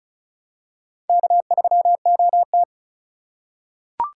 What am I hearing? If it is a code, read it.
Morse code, heard as a keyed tone, about 36 wpm, 700 Hz. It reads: K3OT
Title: USA Call Signs - 4 Characters - 4x Character Spacing - 35wpm